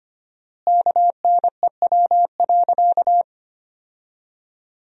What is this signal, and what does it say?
Morse code, heard as a keyed tone, about 25 wpm, 700 Hz. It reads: KNEW.
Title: 4x Word-Spacing - Sentences from Top 400 Words - 25wpm